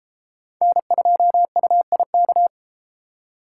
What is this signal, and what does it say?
Morse code, heard as a keyed tone, about 32 wpm, 700 Hz. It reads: N2UIK